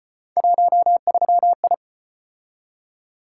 Morse code, heard as a keyed tone, about 34 wpm, 700 Hz. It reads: 13I